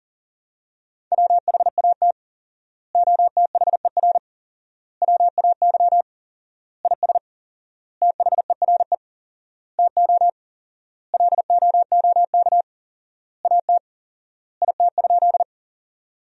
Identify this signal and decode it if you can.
Morse code, heard as a keyed tone, about 40 wpm, 700 Hz. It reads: WHAT OTHER WAY IS THERE TO LOOK AT IT?
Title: 4x Word-Spacing - Sentences from Top 100 Words - 40wpm